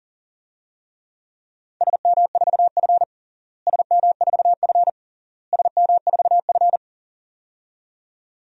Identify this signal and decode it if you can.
Morse code, heard as a keyed tone, about 37 wpm, 700 Hz. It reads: SM4F SM4F SM4F
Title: DXCC-100 Call Signs - 4 Characters - Repeated 3x - 40wpm